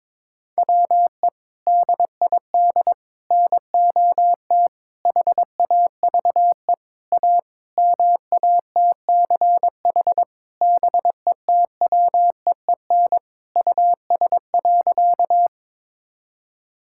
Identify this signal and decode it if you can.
Morse code, heard as a keyed tone, about 22 wpm, 700 Hz. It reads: WE DID NOT HAVE A MATCH BETWEEN US.
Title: Sentences from Top 1600 Words 22wpm